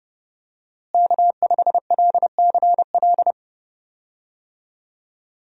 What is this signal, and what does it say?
Morse code, heard as a keyed tone, about 30 wpm, 700 Hz. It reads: K5LCL
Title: USA Call Signs - 5 Characters - Repeated 3x - 30wpm